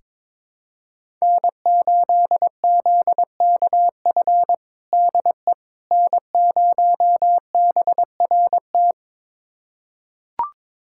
Morse code, heard as a keyed tone, about 22 wpm, 700 Hz. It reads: N8ZKF DE N0BRT
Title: USA Callsign DE Callsign - 5 Characters - 22wpm